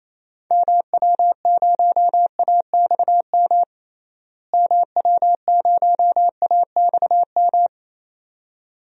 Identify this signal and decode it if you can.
Morse code, heard as a keyed tone, about 28 wpm, 700 Hz. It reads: MW0AXM MW0AXM
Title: DXCC-100 Call Signs - 6 Characters - Repeated 2x - 28wpm